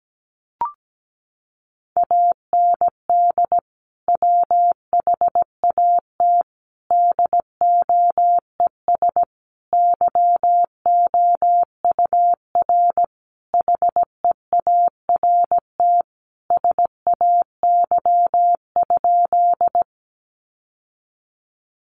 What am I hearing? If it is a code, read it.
Morse code, heard as a keyed tone, about 17 wpm, 700 Hz. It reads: AND WHAT DOES YOUR HEART SAY?
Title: Sentences from Top 1200 Words 17wpm